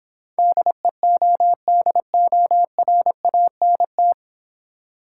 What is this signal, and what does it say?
Morse code, heard as a keyed tone, about 26 wpm, 700 Hz. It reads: DEODORANT